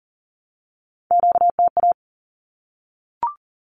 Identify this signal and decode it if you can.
Morse code, heard as a keyed tone, about 40 wpm, 700 Hz. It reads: QTA
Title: Q-Codes 40wpm